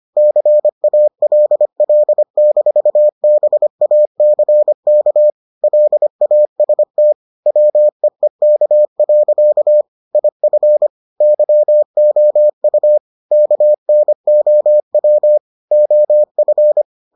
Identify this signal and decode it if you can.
Morse code, heard as a keyed tone, about 25 wpm, 600 Hz. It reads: CALL-BACK LAST WEEK. IF YOU KNOW OF